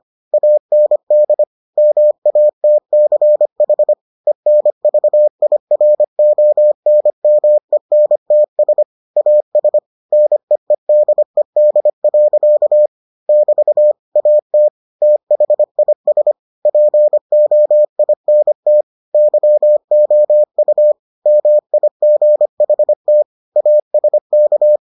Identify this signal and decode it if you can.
Morse code, heard as a keyed tone, about 25 wpm, 600 Hz. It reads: AND MATCH ENVIRONMENTS AS NEEDED. <BT> AT THIS POINT YOU MIGHT ASK